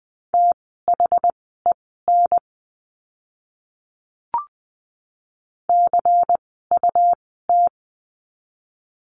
Morse code, heard as a keyed tone, about 20 wpm, spaced as for 15 wpm, 700 Hz. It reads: THEN CUT
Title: 101 Lesson 17f - Letter u - Words 20wpm